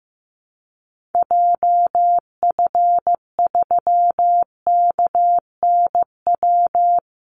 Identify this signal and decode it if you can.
Morse code, heard as a keyed tone, about 15 wpm, 700 Hz. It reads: JF3KNW
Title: Call Signs Repeated 3x - 15wpm